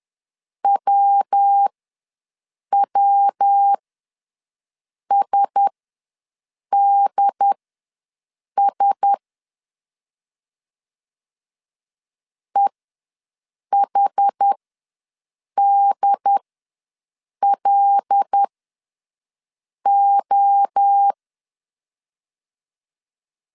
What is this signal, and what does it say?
Morse code, heard as a keyed tone, about 10 wpm, spaced as for 6 wpm, 800 Hz. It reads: WWSDS EHDLO
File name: LEZIONI / 5 LBVC / 5 L ws 10wpm 3636